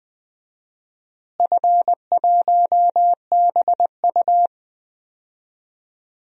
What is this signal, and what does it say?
Morse code, heard as a keyed tone, about 20 wpm, 700 Hz. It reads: F1BU
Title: DXCC-100 Call Signs - 4 Characters - Repeated 3x - 20wpm